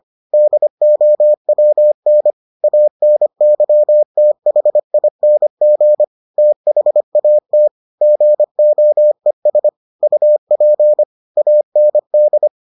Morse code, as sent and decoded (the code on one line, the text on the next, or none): -.. --- .-- -. / .- -. -.-- - .... .. -. --. / - .... .- - / --. --- . ... / ..- .--. / .- -. -..
DOWN ANYTHING THAT GOES UP AND